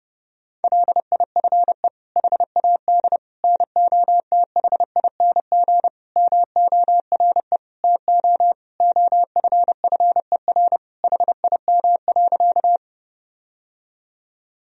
Morse code, heard as a keyed tone, about 30 wpm, 700 Hz. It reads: LIFE HAD NOTHING MORE TO OFFER HIM.
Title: Sentences from Top 1000 Words - Review Part 4 - 30wpm